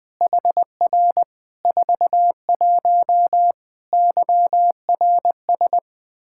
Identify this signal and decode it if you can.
Morse code, heard as a keyed tone, about 20 wpm, 700 Hz. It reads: HR 41 YRS
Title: Practice QSO - Exchange 3 - More Info - 20wpm